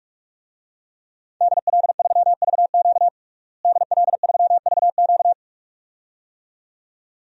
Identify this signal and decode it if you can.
Morse code, heard as a keyed tone, about 46 wpm, 700 Hz. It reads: DL3VQ DL3VQ
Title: DXCC-100 Call Signs - 5 Characters - Repeated 2x - 45wpm